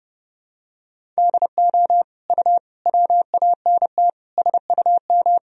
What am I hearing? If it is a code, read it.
Morse code, heard as a keyed tone, about 30 wpm, 700 Hz. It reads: DO U WANT SUM
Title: Sentences from Top 300 Words and Abbreviations 30wpm